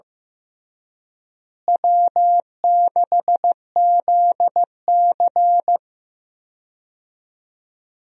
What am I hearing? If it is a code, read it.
Morse code, heard as a keyed tone, about 15 wpm, 700 Hz. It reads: W6ZC
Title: USA Call Signs - 4 Characters - Repeated 2x - 15wpm